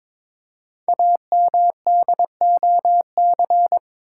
Morse code, heard as a keyed tone, about 22 wpm, 700 Hz. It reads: AMDOC
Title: Code Groups - 5 Letters - 22wpm